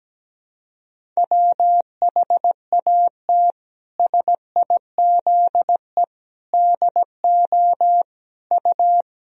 Morse code, heard as a keyed tone, about 17 wpm, 700 Hz. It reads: WHAT SIZE DO U
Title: Sentences from Top 400 Words and Abbreviations 17wpm